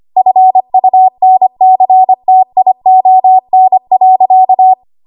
Morse code, heard as a keyed tone, about 25 wpm, 750 Hz. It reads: FUNCTION.
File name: Archive / 25 WPM / 25WPM